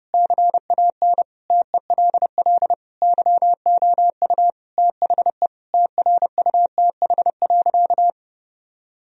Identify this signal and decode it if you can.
Morse code, heard as a keyed tone, about 32 wpm, 700 Hz. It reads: CAN TELL YOU THE TRUTH.
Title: Sentences from Top 1000 Words - Review Part 2 - 30wpm